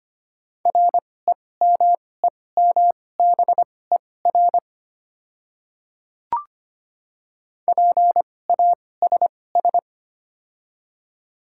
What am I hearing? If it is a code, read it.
Morse code, heard as a keyed tone, about 25 wpm, spaced as for 18 wpm, 700 Hz. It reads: REMEMBER PASS
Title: 2x Character Spacing - Top 500 Words - 25wpm